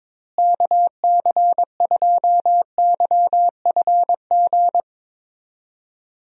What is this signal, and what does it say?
Morse code, heard as a keyed tone, about 22 wpm, 700 Hz. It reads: KC2YFG
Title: USA Call Signs - 6 Characters - Repeated 2x - 22wpm